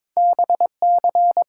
Morse code, heard as a keyed tone, about 22 wpm, 700 Hz. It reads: BC
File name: Top 90 QSO Elements - 8x Character Spacing - 22wpm